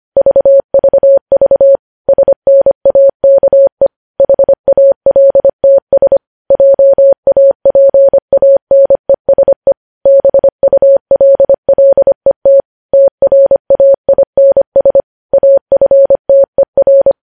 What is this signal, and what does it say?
Morse code, heard as a keyed tone, about 25 wpm, 550 Hz. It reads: VVV SNAKE HALTS JAPANESE BULLET TRAINS AFTER
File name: podcast / News headlines-1x-25x25